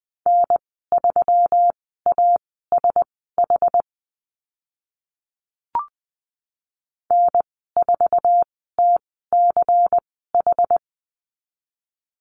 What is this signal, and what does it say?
Morse code, heard as a keyed tone, about 20 wpm, spaced as for 15 wpm, 700 Hz. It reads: N3ASH N4TCH
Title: 238 Lesson 38g - Letter q - Call Signs 20wpm